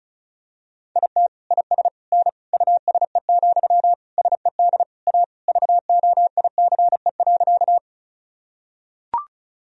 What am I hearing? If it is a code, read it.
Morse code, heard as a keyed tone, about 36 wpm, 700 Hz. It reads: IT IS N USE, SED A VOICE.
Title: Sentences from Top 500 Words and Abbreviations 35wpm